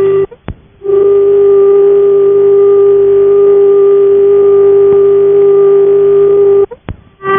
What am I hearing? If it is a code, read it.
Dial tone (Japan), 400 Hz.